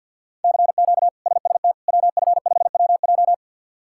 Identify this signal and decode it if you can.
Morse code, heard as a keyed tone, about 53 wpm, 700 Hz. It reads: CQ SST WF5WJ
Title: K1USN SST QSO Practice - 50wpm